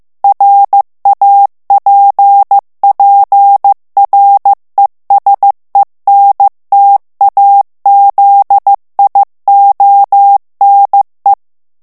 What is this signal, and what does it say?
Morse code, heard as a keyed tone, about 15 wpm, 800 Hz. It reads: RAPPRESENTAZIONE
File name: ITALIEN texte clair 1 vitesse 14 mots minute